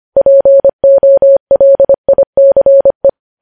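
Morse code, heard as a keyed tone, about 25 wpm, 550 Hz. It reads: POLICE
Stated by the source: Morse code news headlines sent at 25 wpm.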